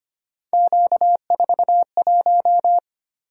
Morse code, heard as a keyed tone, about 25 wpm, 700 Hz. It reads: Q41
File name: Three Letter-Number Combinations - 25wpm